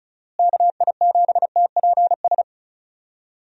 Morse code, heard as a keyed tone, about 34 wpm, 700 Hz. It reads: KI7TPS
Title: USA Call Signs - 6 Characters - Repeated 2x - 35wpm